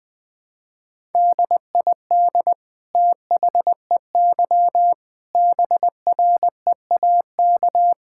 Morse code, heard as a keyed tone, about 20 wpm, 700 Hz. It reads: DID THEY BREAK
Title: Sentences from Top 1000 Words - Review Part 6 - 20wpm